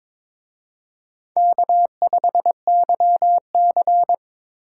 Morse code, heard as a keyed tone, about 26 wpm, 700 Hz. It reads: K5YC
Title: Speed Racing - USA 1x2 Call Signs 22wpm